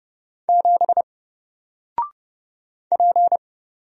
Morse code, heard as a keyed tone, about 31 wpm, 700 Hz. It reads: 7 P
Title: Single Letter-Number Alt 30wpm